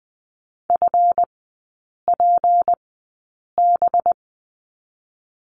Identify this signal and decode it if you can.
Morse code, heard as a keyed tone, about 20 wpm, spaced as for 8 wpm, 700 Hz. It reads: FPB
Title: 205 Lesson 33e - Letter j - Three Characters 20wpm